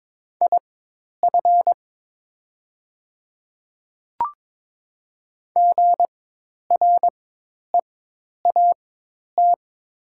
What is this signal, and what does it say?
Morse code, heard as a keyed tone, about 22 wpm, spaced as for 10 wpm, 700 Hz. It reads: IF GREAT
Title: 4x Character Spacing - Top 300 Words - 22wpm